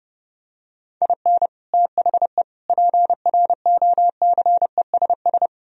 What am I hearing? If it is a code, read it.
Morse code, heard as a keyed tone, about 29 wpm, 700 Hz. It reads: IN THE PROCESS